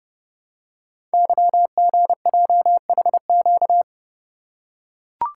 Morse code, heard as a keyed tone, about 30 wpm, 700 Hz. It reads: YGJHQ
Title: Code Groups - 5 Letters - 30wpm